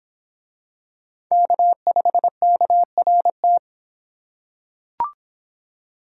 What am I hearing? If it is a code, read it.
Morse code, heard as a keyed tone, about 26 wpm, 700 Hz. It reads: K5KRT